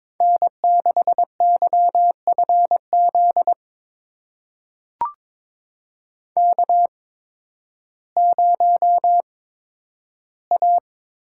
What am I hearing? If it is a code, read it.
Morse code, heard as a keyed tone, about 22 wpm, spaced as for 12 wpm, 700 Hz. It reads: N6YFZ K0A
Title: USA Call Signs - 5 Characters - 8x Character Spacing - 22wpm